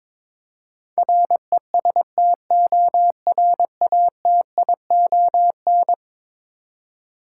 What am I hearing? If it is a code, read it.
Morse code, heard as a keyed tone, about 23 wpm, 700 Hz. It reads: RESTORATION
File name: practice / 22wpm / Eleven Letter Words - 22wpm